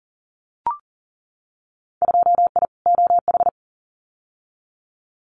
Morse code, heard as a keyed tone, about 37 wpm, 700 Hz. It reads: 2I OH
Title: Winter Field Day Exchanges 40wpm